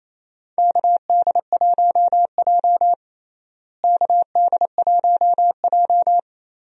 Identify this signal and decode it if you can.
Morse code, heard as a keyed tone, about 28 wpm, 700 Hz. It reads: KD1J KD1J
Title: USA Call Signs - 4 Characters - Repeated 2x - 28wpm